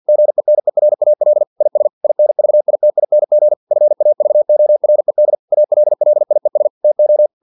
Morse code, heard as a keyed tone, about 50 wpm, 600 Hz. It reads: GENERAL IS INVITING FAVORED ALLIES TO
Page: Never 49wpm